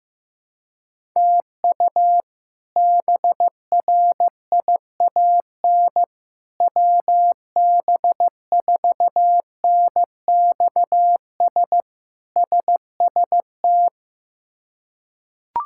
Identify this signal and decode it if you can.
Morse code, heard as a keyed tone, about 15 wpm, 700 Hz. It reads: TU BRIAN WB4NXS SST
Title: K1USN SST QSO Practice - 15wpm